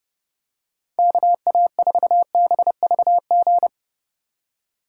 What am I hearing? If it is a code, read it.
Morse code, heard as a keyed tone, about 30 wpm, 700 Hz. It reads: KA4BVG
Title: K1USN SST QSO Practice - 30wpm